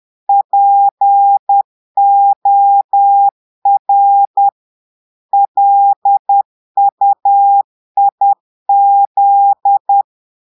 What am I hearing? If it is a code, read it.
Morse code, heard as a keyed tone, about 10 wpm, 800 Hz. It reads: POR LUIZ
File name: Portugais - texte en clair 1 - vitesse - 10 mots minute